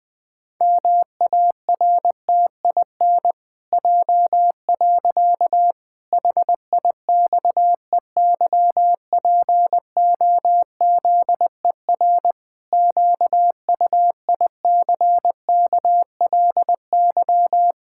Morse code, heard as a keyed tone, about 20 wpm, 700 Hz. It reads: MARTIN J. HIXEYPOZER QUICKLY
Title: Pangram Challenge v2 - 20wpm